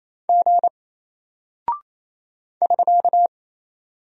Morse code, heard as a keyed tone, about 27 wpm, 700 Hz. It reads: G <SK>
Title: Single Character (Letter:Number:Punctuation:Prosign) 28wpm